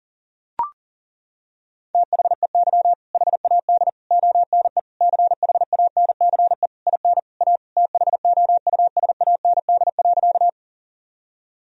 Morse code, heard as a keyed tone, about 40 wpm, 700 Hz. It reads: THEY HAD ONE CHANCE IN A THOUSAND.
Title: Sentences from Top 1000 Words - Review Part 3 - 40wpm